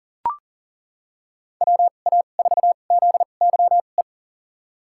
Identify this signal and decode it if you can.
Morse code, heard as a keyed tone, about 37 wpm, spaced as for 28 wpm, 700 Hz. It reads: WA4ZYE
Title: USA Call Signs - 2x Character Spacing - 40wpm